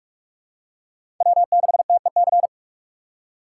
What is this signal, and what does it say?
Morse code, heard as a keyed tone, about 48 wpm, 700 Hz. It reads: W6TEC
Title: USA Call Signs - 5 Characters - 6x Character Spacing - 45wpm